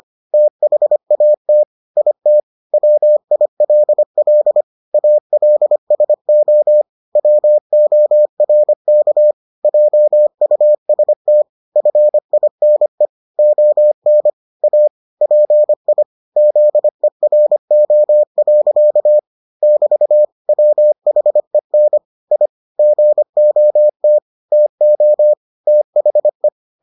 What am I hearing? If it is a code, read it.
Morse code, heard as a keyed tone, about 25 wpm, 600 Hz. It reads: THAT IT WILL ALSO WORK JUST FINE ON A PI ZERO. <BT> WHEN I GOT TO THE